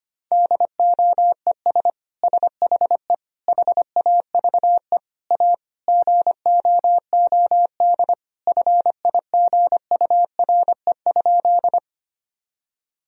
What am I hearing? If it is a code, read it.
Morse code, heard as a keyed tone, about 25 wpm, 700 Hz. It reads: DOES SHE HAVE A GOOD FIGURE?
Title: Sentences from Top 1000 Words - Review Part 2 - 25wpm